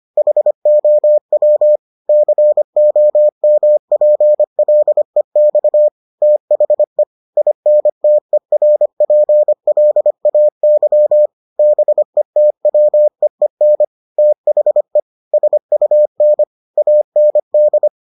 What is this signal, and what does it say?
Morse code, heard as a keyed tone, about 25 wpm, 600 Hz. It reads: HOW COMPLEX THE INTERPLAY BETWEEN THE SUN AND